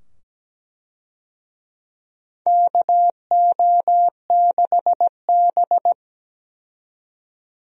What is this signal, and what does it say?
Morse code, heard as a keyed tone, about 17 wpm, 700 Hz. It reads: KO6B
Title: USA Call Signs - 4 Characters - 17wpm